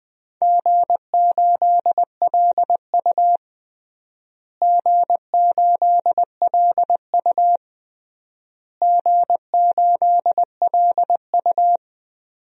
Morse code, heard as a keyed tone, about 20 wpm, 700 Hz. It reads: G8LU G8LU G8LU
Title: DXCC-100 Call Signs - 4 Characters - Repeated 3x - 20wpm